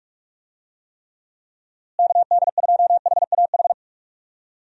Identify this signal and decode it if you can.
Morse code, heard as a keyed tone, about 46 wpm, 700 Hz. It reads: KD2HAH
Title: USA Call Signs - 6 Characters - Repeated 3x - 45wpm